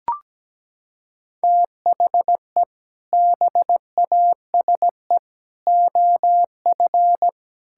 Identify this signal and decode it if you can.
Morse code, heard as a keyed tone, about 17 wpm, 700 Hz. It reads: THE BASE OF
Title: Sentences from Top 1000 Words - Review Part 5 - 17wpm